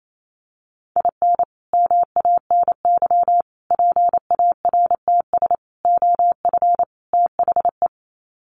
Morse code, heard as a keyed tone, about 28 wpm, 700 Hz. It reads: IN MANY PARTS OF THE
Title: Sets of 5 Words Encore 28wpm